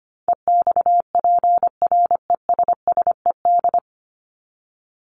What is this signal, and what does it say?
Morse code, heard as a keyed tone, about 25 wpm, 700 Hz. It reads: EXPRESSED
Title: ED Suffix - Top 100 Words with Base Words 25wpm